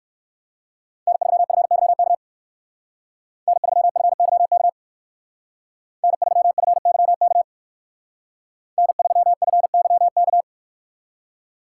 Morse code, heard as a keyed tone, about 56 wpm, 700 Hz. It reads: N3FYK N3FYK N3FYK N3FYK
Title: Speed Racing - USA 1x3 Call Signs 45wpm